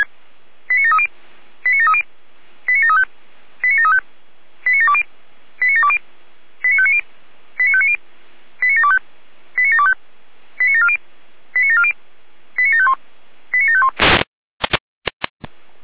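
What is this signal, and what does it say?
Five-tone selective call (ZVEI), heard as quick runs of five tones, tones of about 70 ms. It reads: 89730 89730 89735 89735 89720 89720 89590 89590 89725 89725 89740 89740 89751 89751
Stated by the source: ZVEI five-tone selective call 89705 received with an SDR.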